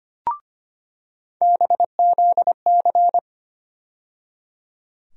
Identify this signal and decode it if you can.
Morse code, heard as a keyed tone, about 25 wpm, 700 Hz. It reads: BZC
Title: USA Call Sign Postfixes - 3 Letters 25wpm